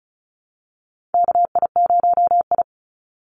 Morse code, heard as a keyed tone, about 34 wpm, 700 Hz. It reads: KI0I
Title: Call Signs Repeated 2x - 35wpm